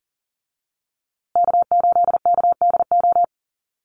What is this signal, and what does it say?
Morse code, heard as a keyed tone, about 39 wpm, 700 Hz. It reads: K8KDO
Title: USA Call Signs Repeated 2x 40wpm